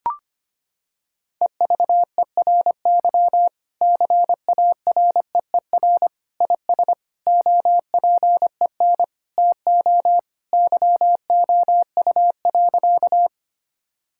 Morse code, heard as a keyed tone, about 25 wpm, 700 Hz. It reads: EVERY CAREER IS OPEN TO YOU.
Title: Sentences from Top 1000 Words - Review Part 6 - 25wpm